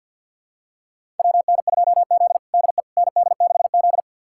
Morse code, heard as a keyed tone, about 46 wpm, 700 Hz. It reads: WN2Z DE ND6Z
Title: USA Callsign DE Callsign - 4 Characters - 50wpm